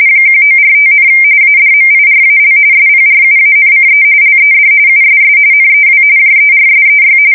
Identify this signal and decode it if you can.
RTTY, about 45.45 baud. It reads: 1945, ADMIRAL SPRUANCE LED USS NEW JERSEY
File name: fsk170 / USS-New-Jersey